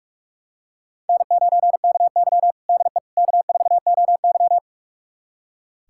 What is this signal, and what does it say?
Morse code, heard as a keyed tone, about 42 wpm, 700 Hz. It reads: N9KY DE K4OY
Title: USA Callsign DE Callsign - 4 Characters - 45wpm